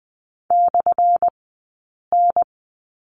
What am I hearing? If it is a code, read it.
Morse code, heard as a keyed tone, about 20 wpm, spaced as for 8 wpm, 700 Hz. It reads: /N
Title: 177 Lesson 29d - Slash - Two Characters 20wpm